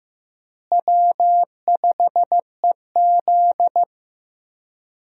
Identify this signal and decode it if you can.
Morse code, heard as a keyed tone, about 14 wpm, 700 Hz. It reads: W5EZ
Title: USA Call Signs - 4 Characters - Repeated 3x - 15wpm